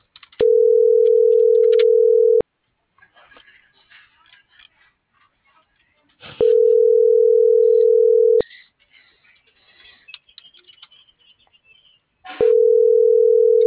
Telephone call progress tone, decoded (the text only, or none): ringback tone